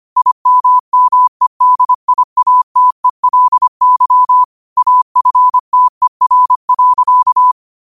Morse code, heard as a keyed tone, about 25 wpm, 1000 Hz. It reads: IMMEDIATELY AFTER.
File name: qotdfiles / 25wpm